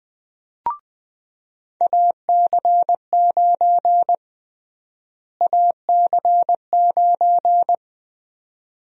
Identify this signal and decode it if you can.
Morse code, heard as a keyed tone, about 20 wpm, 700 Hz. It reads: AC9 AC9
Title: USA Call Sign Prefixes - Repeated 2x - 20wpm